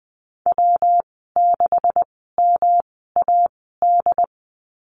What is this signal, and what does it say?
Morse code, heard as a keyed tone, about 20 wpm, spaced as for 15 wpm, 700 Hz. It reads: W6MAD
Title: 128 Lesson 21g - Number 6 - Call Signs 20wpm